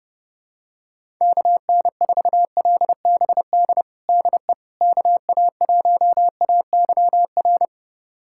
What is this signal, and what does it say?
Morse code, heard as a keyed tone, about 31 wpm, 700 Hz. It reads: KN4LBD DE KA1AYR
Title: USA Callsign DE Callsign - 6 Characters - 30wpm